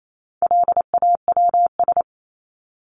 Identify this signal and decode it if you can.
Morse code, heard as a keyed tone, about 28 wpm, 700 Hz. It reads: LAWS
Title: ES:S Suffix - Top 100 Words with Base Words 28wpm